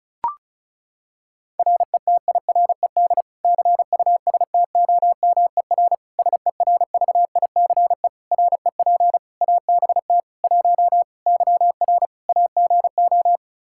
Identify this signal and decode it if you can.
Morse code, heard as a keyed tone, about 34 wpm, 700 Hz. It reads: RETIRED CUSTOMER SERVICE REP ABT 1 YR AGO
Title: Practice QSO - Exchange 3 - More Info - 35wpm